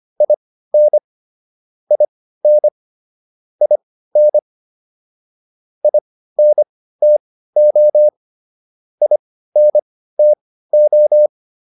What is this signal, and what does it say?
Morse code, heard as a keyed tone, about 25 wpm, spaced as for 15 wpm, 600 Hz. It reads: IN IN IN INTO INTO